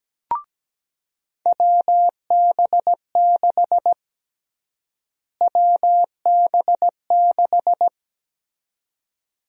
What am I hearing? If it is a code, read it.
Morse code, heard as a keyed tone, about 17 wpm, 700 Hz. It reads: WB6 WB6
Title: USA Call Sign Prefixes - Repeated 2x - 17wpm